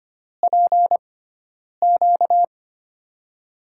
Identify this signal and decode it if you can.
Morse code, heard as a keyed tone, about 25 wpm, spaced as for 9 wpm, 700 Hz. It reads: PQ
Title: USA Call Signs - 6 Characters - 6x Character Spacing - 25wpm